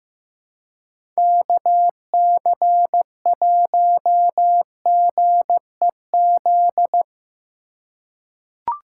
Morse code, heard as a keyed tone, about 15 wpm, 700 Hz. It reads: KC1GEZ
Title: USA Call Signs - 6 Characters - 2x Character Spacing - 15wpm